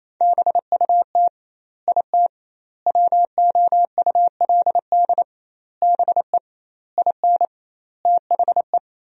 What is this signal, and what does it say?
Morse code, heard as a keyed tone, about 28 wpm, 700 Hz. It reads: BUT IT WOULD BE IN THE
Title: 2x Word-Spacing - Sentences from Top 300 Words - 28wpm